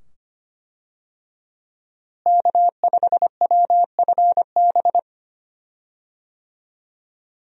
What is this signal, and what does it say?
Morse code, heard as a keyed tone, about 25 wpm, 700 Hz. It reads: K5WFB
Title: USA Call Signs - 5 Characters - Repeated 2x - 25wpm